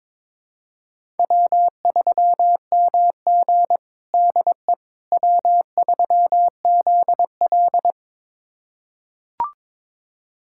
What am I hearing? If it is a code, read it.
Morse code, heard as a keyed tone, about 22 wpm, 700 Hz. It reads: W3MG DE W3ZL
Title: USA Callsign DE Callsign - 4 Characters - 22wpm